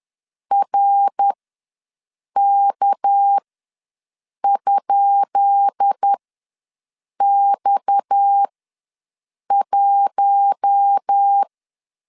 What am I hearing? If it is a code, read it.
Morse code, heard as a keyed tone, about 10 wpm, spaced as for 6 wpm, 800 Hz. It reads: RK?X1
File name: LEZIONI / 10 SIGNS / 10B INT ws 10wpm 4134